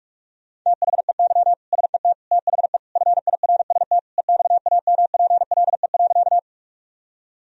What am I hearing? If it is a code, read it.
Morse code, heard as a keyed tone, about 46 wpm, 700 Hz. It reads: THEY SET THE FIRST EXAMPLE.
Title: Sentences from Top 1000 Words - Review Part 1 - 45wpm